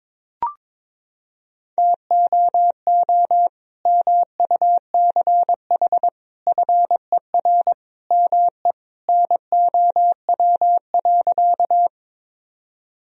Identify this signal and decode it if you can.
Morse code, heard as a keyed tone, about 22 wpm, 700 Hz. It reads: TOO MUCH FER ME NOW.
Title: Sentences from Top 200 Words and Abbreviations 22wpm